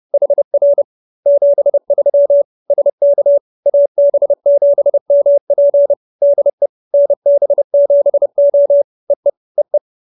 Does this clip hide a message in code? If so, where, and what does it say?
Yes: Morse code from 0.1 to 9.8 s, heard as a keyed tone, about 30 wpm, 570 Hz. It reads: HR 73 SK AB7MP DE NB7O EE EE